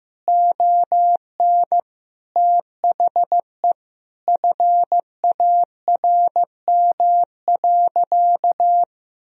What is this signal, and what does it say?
Morse code, heard as a keyed tone, about 15 wpm, 700 Hz. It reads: ON THE FARM.
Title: Sentences from Top 1000 Words - Review Part 2 - 15wpm